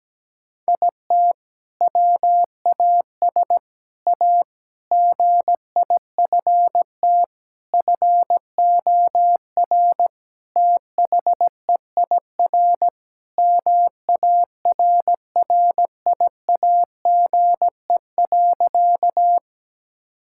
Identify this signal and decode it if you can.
Morse code, heard as a keyed tone, about 17 wpm, 700 Hz. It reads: IT WAS A GIFT FOR THEIR MARRIAGE.
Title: Sentences from Top 1000 Words - Review Part 6 - 17wpm